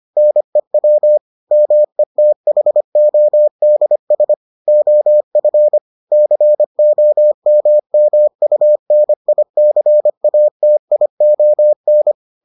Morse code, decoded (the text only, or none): NEW METHODS OF COMMUNICATION